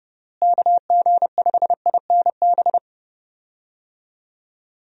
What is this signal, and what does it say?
Morse code, heard as a keyed tone, about 30 wpm, 700 Hz. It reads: KG5INB
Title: USA Call Signs - 6 Characters - Repeated 2x - 30wpm